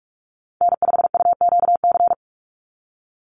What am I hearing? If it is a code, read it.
Morse code, heard as a keyed tone, about 44 wpm, 700 Hz. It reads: N5UQC
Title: USA Call Signs 45wpm